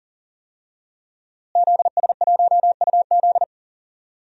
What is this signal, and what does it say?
Morse code, heard as a keyed tone, about 40 wpm, 700 Hz. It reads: ZS1UZ
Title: DXCC-100 Call Signs - 5 Characters - Repeated 2x - 40wpm